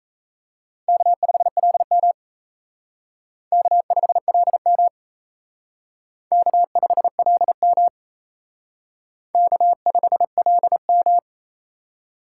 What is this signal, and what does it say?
Morse code, heard as a keyed tone, about 36 wpm, 700 Hz. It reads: K5LM K5LM K5LM K5LM